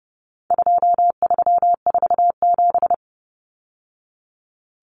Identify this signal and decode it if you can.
Morse code, heard as a keyed tone, about 30 wpm, 700 Hz. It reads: 2347
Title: Numbers - 4-Digit 30wpm